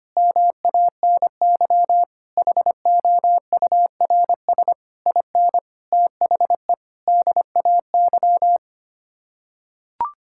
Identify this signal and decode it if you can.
Morse code, heard as a keyed tone, about 25 wpm, 700 Hz. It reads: MANY HOURS IN THE DAY
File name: Sets of 9 Words - 25wpm